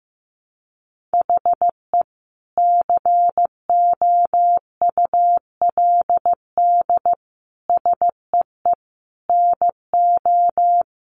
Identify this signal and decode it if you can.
Morse code, heard as a keyed tone, about 15 wpm, 700 Hz. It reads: HE COULD SEE NO
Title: Sentences from Top 1200 Words 15wpm